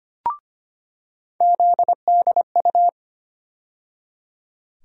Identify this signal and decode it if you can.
Morse code, heard as a keyed tone, about 25 wpm, 700 Hz. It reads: ZDU
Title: USA Call Sign Postfixes - 3 Letters 25wpm